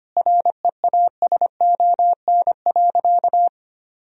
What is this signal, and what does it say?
Morse code, heard as a keyed tone, about 25 wpm, 700 Hz. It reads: REASON.
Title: Sentences from Top 1000 Words - Review Part 3 - 25wpm